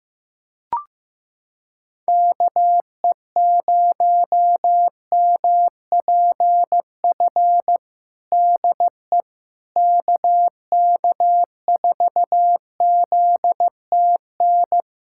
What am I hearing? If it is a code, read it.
Morse code, heard as a keyed tone, about 15 wpm, 700 Hz. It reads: KE0MPF DE KK4ZTN
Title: Practice QSO - Exchange 4 - Ending - 15wpm